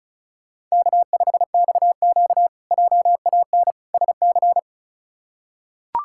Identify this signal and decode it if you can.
Morse code, heard as a keyed tone, about 36 wpm, 700 Hz. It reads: K5XQ JAN SC
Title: K1USN SST QSO Practice - 35wpm